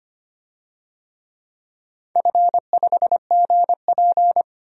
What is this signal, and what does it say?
Morse code, heard as a keyed tone, about 25 wpm, 700 Hz. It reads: F5GP
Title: DXCC-100 Call Signs - 4 Characters - 25wpm